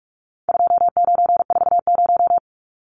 Morse code, heard as a keyed tone, about 44 wpm, 700 Hz. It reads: 2940
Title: Numbers - 4-Digit 45wpm